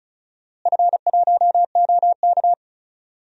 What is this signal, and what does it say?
Morse code, heard as a keyed tone, about 36 wpm, 700 Hz. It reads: F1OK
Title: DXCC-100 Call Signs - 4 Characters - Repeated 3x - 35wpm